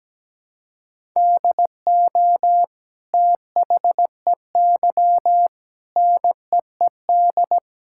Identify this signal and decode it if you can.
Morse code, heard as a keyed tone, about 17 wpm, 700 Hz. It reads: DO THEY NEED
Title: Sentences from Top 1000 Words - Review Part 5 - 17wpm